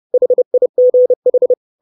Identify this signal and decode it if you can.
Morse code, heard as a keyed tone, about 30 wpm, 500 Hz. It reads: HIGH
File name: 632 Words 30WPM